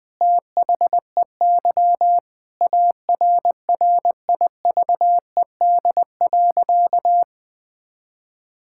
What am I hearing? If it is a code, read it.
Morse code, heard as a keyed tone, about 21 wpm, 700 Hz. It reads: THEY ARRIVED.
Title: Sentences with 2-Word Phrases - 20wpm